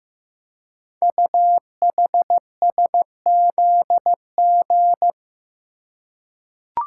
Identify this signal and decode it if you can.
Morse code, heard as a keyed tone, about 15 wpm, 700 Hz. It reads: UHSZG
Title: Code Groups - 5 Letters - 15wpm